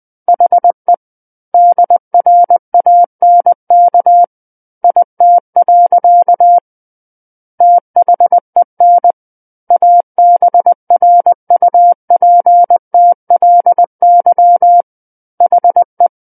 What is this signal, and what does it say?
Morse code, heard as a keyed tone, about 20 wpm, 700 Hz. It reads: HE DRANK IT. THEN ABRUPTLY HE
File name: 20 WPM